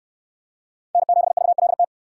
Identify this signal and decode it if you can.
Morse code, heard as a keyed tone, about 66 wpm, 700 Hz. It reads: N7VLT